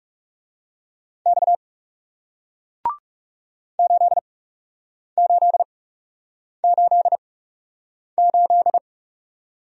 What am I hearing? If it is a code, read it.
Morse code, heard as a keyed tone, about 39 wpm, spaced as for 8 wpm, 700 Hz. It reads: X 8888